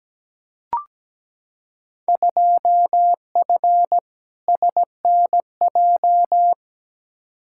Morse code, heard as a keyed tone, about 17 wpm, 700 Hz. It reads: 2F SNJ
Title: Field Day Exchanges Encore - 17wpm